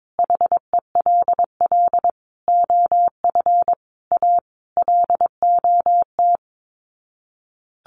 Morse code, heard as a keyed tone, about 22 wpm, 700 Hz. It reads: HELL OF A LOT
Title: Sets of 5 Words Encore 22wpm